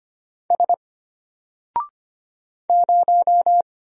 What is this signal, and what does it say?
Morse code, heard as a keyed tone, about 25 wpm, 700 Hz. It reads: S 0
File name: Single-Letter-Number-25wpm